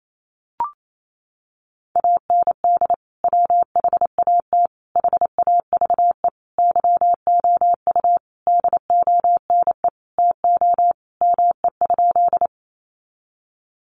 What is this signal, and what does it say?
Morse code, heard as a keyed tone, about 28 wpm, 700 Hz. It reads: AND WHAT HAVE YOU DONE TO ME?
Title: Sentences from Top 500 Words 28wpm